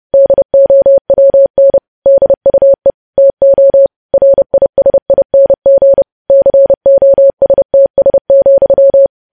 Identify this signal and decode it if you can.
Morse code, heard as a keyed tone, about 30 wpm, 550 Hz. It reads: DOWN DUE TO RISING COSTS,
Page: news headlines in morse code at 30 wpm